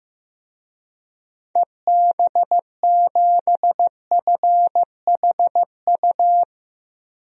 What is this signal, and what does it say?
Morse code, heard as a keyed tone, about 15 wpm, 700 Hz. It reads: EB7FHU
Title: DXCC-100 Call Signs - 6 Characters - 15wpm